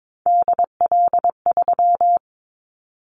Morse code, heard as a keyed tone, about 22 wpm, 700 Hz. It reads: DL3
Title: Callsign Prefixes for Top 50 DXCC - 22wpm